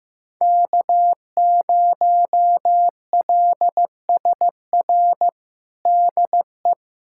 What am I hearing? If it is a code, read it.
Morse code, heard as a keyed tone, about 15 wpm, 700 Hz. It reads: K0LSR DE
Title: USA Callsign DE Callsign - 5 Characters - 15wpm